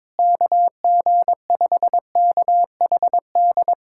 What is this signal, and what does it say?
Morse code, heard as a keyed tone, about 21 wpm, 700 Hz. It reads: KG5KHD
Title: USA Callsign DE Callsign 22wpm